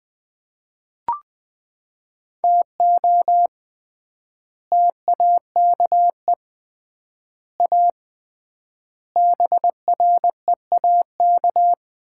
Morse code, heard as a keyed tone, about 20 wpm, 700 Hz. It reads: TO TAKE A BREAK
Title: 3x Word-Spacing - Sets of 4 Words - 20wpm